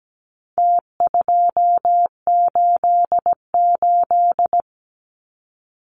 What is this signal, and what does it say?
Morse code, heard as a keyed tone, about 17 wpm, 700 Hz. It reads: T288
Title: Numbers - Contest Style - 4-Digit 17wpm